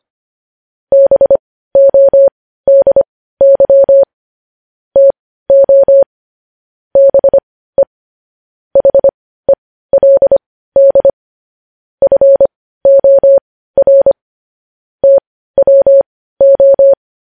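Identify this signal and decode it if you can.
Morse code, heard as a keyed tone, about 25 wpm, spaced as for 15 wpm, 550 Hz. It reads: BODY TO BE HELD FOR TWO
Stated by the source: Morse news headlines practice at 15.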